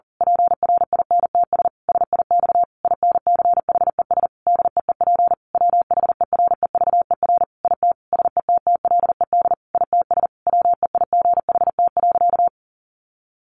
Morse code, heard as a keyed tone, about 37 wpm, 700 Hz. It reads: PRINTS SIX INCHES DEEP WHEREVER IT SETTLED ITS WEIGHT.
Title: A Sound Of Thunder 40wpm